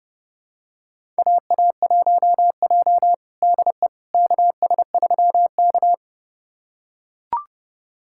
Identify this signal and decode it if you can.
Morse code, heard as a keyed tone, about 30 wpm, 700 Hz. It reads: AA1J DE KS3K
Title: USA Callsign DE Callsign - 4 Characters - 30wpm